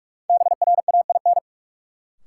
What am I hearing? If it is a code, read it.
Morse code, heard as a keyed tone, about 46 wpm, 700 Hz. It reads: BRAIN
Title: Sentences from Top 1100 Words - New Word Drill Alt - 45wpm